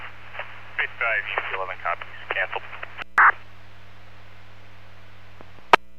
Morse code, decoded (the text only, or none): none